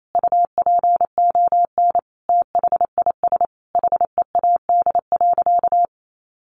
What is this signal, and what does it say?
Morse code, heard as a keyed tone, about 28 wpm, 700 Hz. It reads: UPON THIS HEAD.
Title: Sentences from Top 1500 Words 28wpm